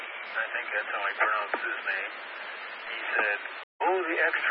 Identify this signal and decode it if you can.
Emergency Alert System (SAME) tones, heard as harsh buzzing bursts, with speech. It is heard from 0.3 to 3.5 s.